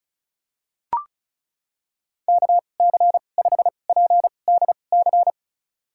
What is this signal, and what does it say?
Morse code, heard as a keyed tone, about 34 wpm, spaced as for 25 wpm, 700 Hz. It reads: KC5PDC
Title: USA Call Signs - 2x Character Spacing - 35wpm